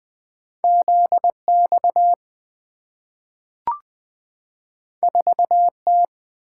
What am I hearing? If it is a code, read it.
Morse code, heard as a keyed tone, about 20 wpm, 700 Hz. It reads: ZX 4T
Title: Two Letter-Number Combinations - 20wpm